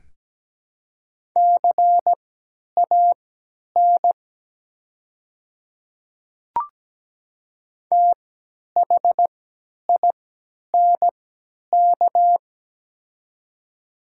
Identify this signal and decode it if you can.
Morse code, heard as a keyed tone, about 17 wpm, spaced as for 10 wpm, 700 Hz. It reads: CAN THINK
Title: 3x Character Spacing - Top 200 Words - 17wpm